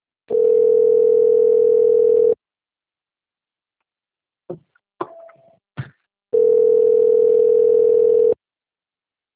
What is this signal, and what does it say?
Ringback tone (North America), 440 + 480 Hz.